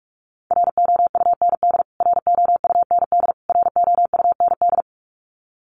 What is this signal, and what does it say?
Morse code, heard as a keyed tone, about 44 wpm, 700 Hz. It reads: ROUND ROUND ROUND
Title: Top 200 Words Repeated 45wpm